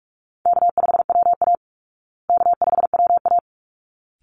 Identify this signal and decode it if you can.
Morse code, heard as a keyed tone, about 44 wpm, 700 Hz. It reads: K5WA K5WA
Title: Easy Call Signs Repeated 2x - 45wpm